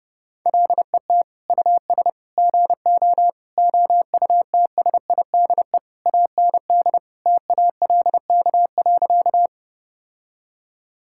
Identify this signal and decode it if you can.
Morse code, heard as a keyed tone, about 30 wpm, 700 Hz. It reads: LET US GO OUTSIDE AND TALK.
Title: Sentences from Top 1000 Words - Review Part 6 - 30wpm